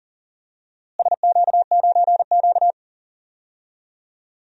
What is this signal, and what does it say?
Morse code, heard as a keyed tone, about 40 wpm, 700 Hz. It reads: SQ9Q
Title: DXCC-100 Call Signs - 4 Characters - Repeated 3x - 40wpm